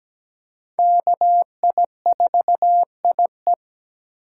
Morse code, heard as a keyed tone, about 17 wpm, 700 Hz. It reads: KI4IE
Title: USA Call Signs - 5 Characters - 3x Character Spacing - 17wpm